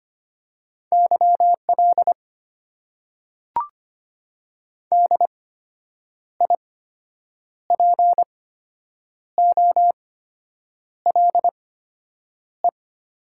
Morse code, heard as a keyed tone, about 25 wpm, spaced as for 8 wpm, 700 Hz. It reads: YL DIPOLE
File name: Top 90 QSO Elements - 8x Character Spacing - 25wpm